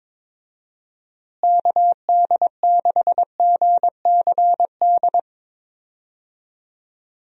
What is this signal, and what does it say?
Morse code, heard as a keyed tone, about 22 wpm, 700 Hz. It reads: KD6GCD
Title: USA Call Signs - 6 Characters - Repeated 2x - 22wpm